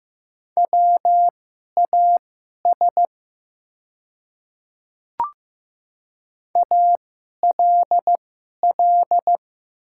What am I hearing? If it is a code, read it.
Morse code, heard as a keyed tone, about 15 wpm, spaced as for 11 wpm, 700 Hz. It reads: WAS ALL
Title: 2x Character Spacing - Top 200 Words - 15wpm